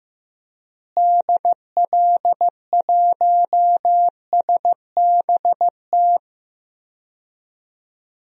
Morse code, heard as a keyed tone, about 15 wpm, 700 Hz. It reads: DL1SBT
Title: DXCC-100 Call Signs - 6 Characters - Repeated 3x - 15wpm